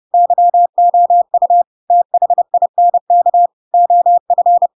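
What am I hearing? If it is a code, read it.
Morse code, heard as a keyed tone, about 30 wpm, 700 Hz. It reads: YOU THINK OF
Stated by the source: Chapter 2 of The Adventures of Pinocchio read as a book in Morse code (CW) at 30 wpm, by ik1hge.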